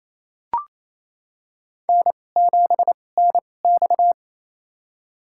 Morse code, heard as a keyed tone, about 28 wpm, spaced as for 20 wpm, 690 Hz. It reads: N7NX
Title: USA Call Signs - 2x Character Spacing - 28wpm